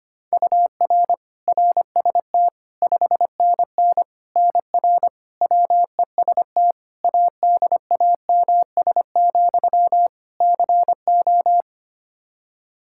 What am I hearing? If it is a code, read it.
Morse code, heard as a keyed tone, about 25 wpm, 700 Hz. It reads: UR RST 5NN NR WEST ADAMS, CO
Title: Practice QSO - Exchange 1 - Beginning - 25wpm